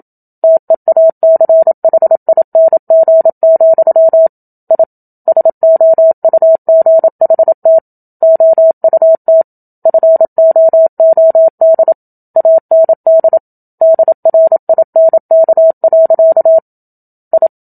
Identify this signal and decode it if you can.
Morse code, heard as a keyed tone, about 27 wpm, 650 Hz. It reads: TEACHING, I SOUGHT OUT FOOD AND DRINK. I
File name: Book Two / 01-Chapter 1 - 27 WPM